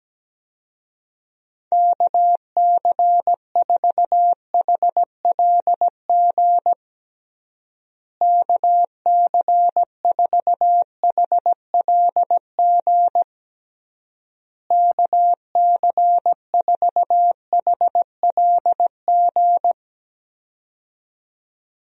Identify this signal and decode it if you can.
Morse code, heard as a keyed tone, about 17 wpm, 700 Hz. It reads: KC4HLG KC4HLG KC4HLG
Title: USA Call Signs - 6 Characters - Repeated 3x - 17wpm